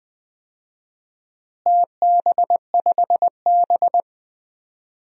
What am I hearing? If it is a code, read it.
Morse code, heard as a keyed tone, about 20 wpm, 700 Hz. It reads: TB5B
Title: DXCC-100 Call Signs - 4 Characters - Repeated 3x - 20wpm